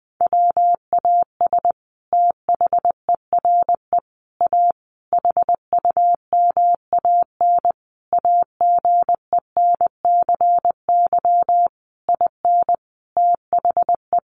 Morse code, heard as a keyed tone, about 20 wpm, 700 Hz. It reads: WAS THERE A HUMAN AGENCY IN THE
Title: Sentences from Top 1300 Words 20wpm